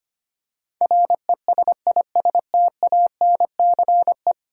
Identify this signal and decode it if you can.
Morse code, heard as a keyed tone, about 25 wpm, 700 Hz. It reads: RESISTANCE
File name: 10 Letter Words - First 3 Letters Given - 25wpm